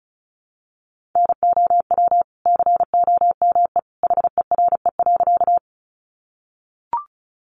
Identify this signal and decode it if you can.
Morse code, heard as a keyed tone, about 34 wpm, 700 Hz. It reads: NOW COME HERE.
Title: Sentences from Top 200 Words 35wpm